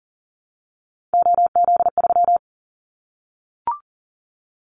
Morse code, heard as a keyed tone, about 40 wpm, 700 Hz. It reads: OZ3
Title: Callsign Prefixes for Top 100 DXCC - 40wpm